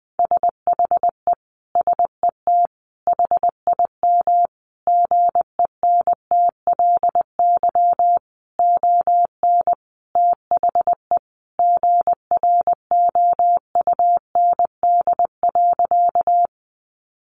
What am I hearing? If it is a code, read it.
Morse code, heard as a keyed tone, about 20 wpm, 700 Hz. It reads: SHE SET HIM GENTLY ON THE GROUND.
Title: Sentences from Top 1500 Words 20wpm